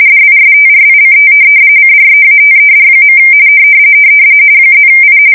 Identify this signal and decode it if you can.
RTTY, about 45.45 baud. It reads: DISTRICT CREATED. +I
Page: 170 Shift 2125/2295